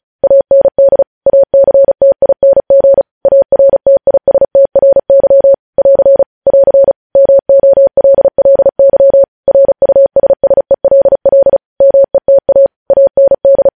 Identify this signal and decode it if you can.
Morse code, heard as a keyed tone, about 36 wpm, 550 Hz. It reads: AND ACTING ARTISTRY <AR> <AR> MOLLY RUSSELL META AND
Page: news headlines in morse code at 35 wpm